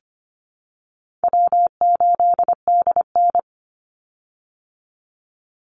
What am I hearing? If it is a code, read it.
Morse code, heard as a keyed tone, about 25 wpm, 700 Hz. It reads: W8DN
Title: Easy Call Signs Repeated 2x - 25wpm